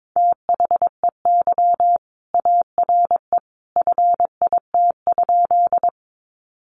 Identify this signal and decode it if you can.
Morse code, heard as a keyed tone, about 22 wpm, 700 Hz. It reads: THEY ARE FIT?
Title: Sentences from Top 800 Words 22wpm